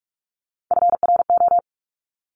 Morse code, heard as a keyed tone, about 46 wpm, 700 Hz. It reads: FRO
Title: Top Three Letter Combinations in Words 45wpm